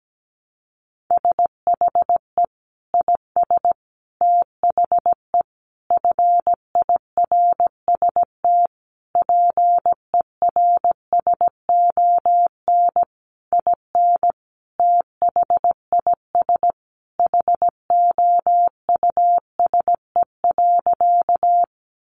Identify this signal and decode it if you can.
Morse code, heard as a keyed tone, about 17 wpm, 700 Hz. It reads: SHE IS THE FIRST PERSON IN THIS HOUSE.
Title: Sentences from Top 400 Words 17wpm